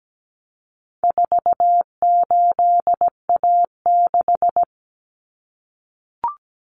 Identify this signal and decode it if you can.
Morse code, heard as a keyed tone, about 17 wpm, 700 Hz. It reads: 48A6
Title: Common Cut Numbers - 4-Digit 17wpm